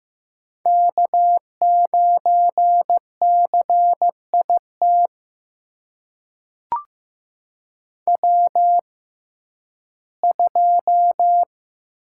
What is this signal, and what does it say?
Morse code, heard as a keyed tone, about 15 wpm, spaced as for 11 wpm, 700 Hz. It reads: K9CIT W2
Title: USA Call Signs - 6x Character Spacing - 15wpm